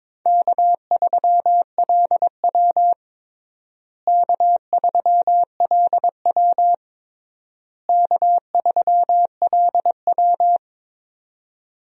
Morse code, heard as a keyed tone, about 22 wpm, 700 Hz. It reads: K3LW K3LW K3LW
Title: USA Call Signs - 4 Characters - Repeated 3x - 22wpm